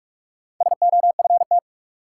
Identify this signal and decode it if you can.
Morse code, heard as a keyed tone, about 46 wpm, 700 Hz. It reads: SOFT
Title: Sentences from Top 1500 Words - New Word Drill Alt - 45wpm